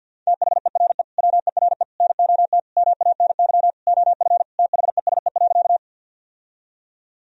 Morse code, heard as a keyed tone, about 52 wpm, 700 Hz. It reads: THERE WERE NOT MANY OF THESE.
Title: Sentences from Top 1000 Words - Review Part 1 - 50wpm